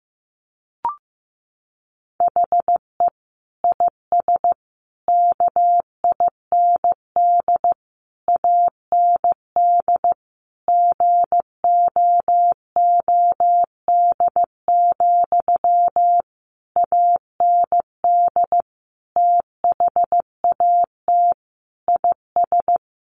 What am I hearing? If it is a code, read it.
Morse code, heard as a keyed tone, about 15 wpm, 700 Hz. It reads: HE IS KIND AND GOOD, AND THAT IS
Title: Sentences from Top 1400 Words 15wpm